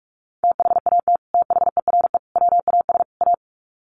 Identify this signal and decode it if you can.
Morse code, heard as a keyed tone, about 44 wpm, 700 Hz. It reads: THAT THERE WAS A
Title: Sets of 4 Words 45wpm